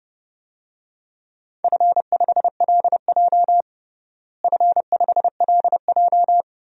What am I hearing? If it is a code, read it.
Morse code, heard as a keyed tone, about 32 wpm, 700 Hz. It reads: F5LJ F5LJ
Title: DXCC-100 Call Signs - 4 Characters - Repeated 2x - 30wpm